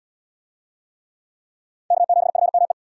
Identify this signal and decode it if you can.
Morse code, heard as a keyed tone, about 74 wpm, 700 Hz. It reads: K7FME